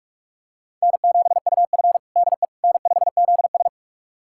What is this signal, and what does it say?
Morse code, heard as a keyed tone, about 41 wpm, 700 Hz. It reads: N7UF DE N5ZS
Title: USA Callsign DE Callsign - 4 Characters - 45wpm